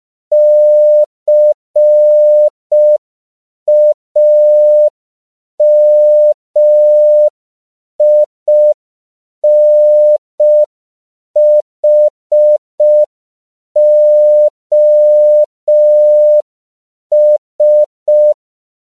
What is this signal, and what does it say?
Morse code, heard as a keyed tone, about 5 wpm, 600 Hz. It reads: CAMINHOS
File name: 5PPM